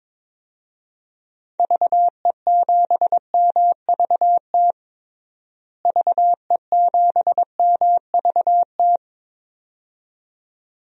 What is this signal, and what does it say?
Morse code, heard as a keyed tone, about 22 wpm, 700 Hz. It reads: VE7MVT VE7MVT
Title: DXCC-100 Call Signs - 6 Characters - Repeated 2x - 22wpm